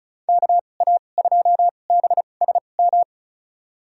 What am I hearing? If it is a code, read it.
Morse code, heard as a keyed tone, about 36 wpm, spaced as for 25 wpm, 700 Hz. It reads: KA2BSM
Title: USA Call Signs - 2x Character Spacing - 35wpm